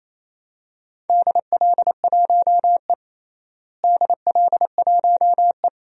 Morse code, heard as a keyed tone, about 28 wpm, 700 Hz. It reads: DL1E DL1E
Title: DXCC-100 Call Signs - 4 Characters - Repeated 2x - 28wpm